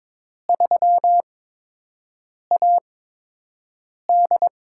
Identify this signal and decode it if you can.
Morse code, heard as a keyed tone, about 23 wpm, spaced as for 6 wpm, 700 Hz. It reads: 3AD
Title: USA Call Signs - 4 Characters - 8x Character Spacing - 22wpm